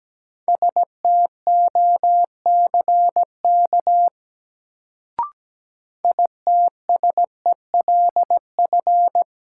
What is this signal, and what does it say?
Morse code, heard as a keyed tone, about 17 wpm, 700 Hz. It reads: STOCK ITSELF
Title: Sentences from Top 800 Words - New Word Drill - 17wpm